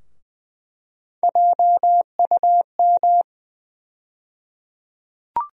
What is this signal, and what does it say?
Morse code, heard as a keyed tone, about 20 wpm, 700 Hz. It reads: JUM
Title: USA Call Sign Postfixes - 3 Letters 20wpm